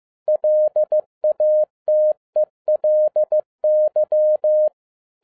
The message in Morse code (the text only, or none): LATELY